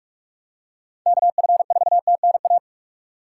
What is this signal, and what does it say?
Morse code, heard as a keyed tone, about 41 wpm, 700 Hz. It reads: KF4TNA